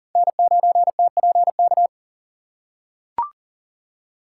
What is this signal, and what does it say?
Morse code, heard as a keyed tone, about 40 wpm, 700 Hz. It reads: N9TPK
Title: USA Call Signs - 5 Characters - 2x Character Spacing - 40wpm